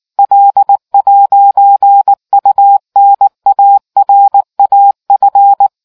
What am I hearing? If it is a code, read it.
Morse code, heard as a keyed tone, about 18 wpm, 800 Hz. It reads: L'UNARAF